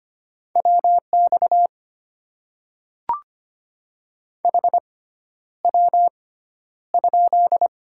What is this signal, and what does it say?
Morse code, heard as a keyed tone, about 25 wpm, spaced as for 11 wpm, 700 Hz. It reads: WX HW?
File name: Top 90 QSO Elements - 6x Character Spacing - 25wpm